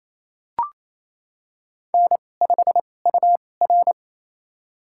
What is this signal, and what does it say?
Morse code, heard as a keyed tone, about 28 wpm, spaced as for 20 wpm, 700 Hz. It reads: N5UR
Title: USA Call Signs - 4 Characters - 2x Character Spacing - 28wpm